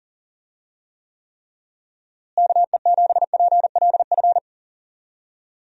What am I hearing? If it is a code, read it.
Morse code, heard as a keyed tone, about 37 wpm, 700 Hz. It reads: KE7PLF
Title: USA Call Signs - 6 Characters - Repeated 3x - 40wpm